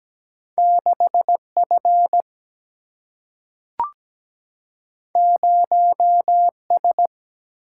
Morse code, heard as a keyed tone, about 17 wpm, 700 Hz. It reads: 6 F 0 S
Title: Single-Letter Single-Number 17wpm